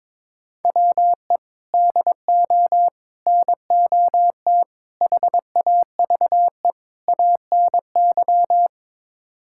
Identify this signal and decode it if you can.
Morse code, heard as a keyed tone, about 22 wpm, 700 Hz. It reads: WE DO NOT HAVE ANY
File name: practice / 22wpm / Build A Phrase - 5-Word Phrases - 22wpm